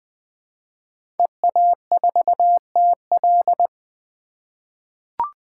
Morse code, heard as a keyed tone, about 20 wpm, 700 Hz. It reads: EA4TL
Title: 2023 POTA Hunter Call Signs - 20wpm